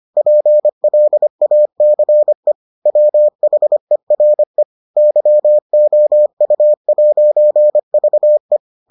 Morse code, heard as a keyed tone, about 25 wpm, 600 Hz. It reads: PLACE WHERE YOU'VE